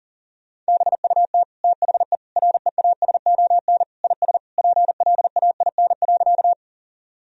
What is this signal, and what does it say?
Morse code, heard as a keyed tone, about 40 wpm, 700 Hz. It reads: BUT THE REASON IS PLAIN.
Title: Sentences from Top 1000 Words - Review Part 2 - 40wpm